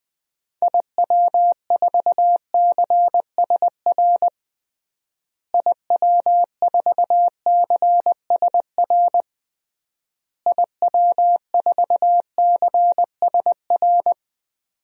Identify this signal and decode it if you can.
Morse code, heard as a keyed tone, about 20 wpm, 700 Hz. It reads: IW4CSR IW4CSR IW4CSR
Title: DXCC-100 Call Signs - 6 Characters - Repeated 3x - 20wpm